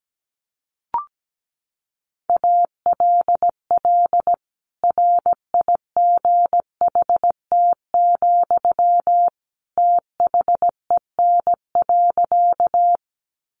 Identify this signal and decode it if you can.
Morse code, heard as a keyed tone, about 17 wpm, 700 Hz. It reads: ALL RIGHT, THEN.
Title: Sentences from Top 200 Words 17wpm